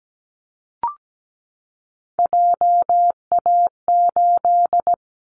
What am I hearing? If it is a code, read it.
Morse code, heard as a keyed tone, about 17 wpm, 700 Hz. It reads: JA8
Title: Callsign Prefixes for Top 50 DXCC - 17wpm